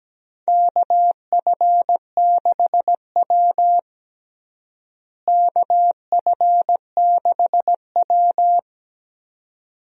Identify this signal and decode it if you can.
Morse code, heard as a keyed tone, about 17 wpm, 700 Hz. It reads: KF6W KF6W
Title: USA Call Signs - 4 Characters - Repeated 2x - 17wpm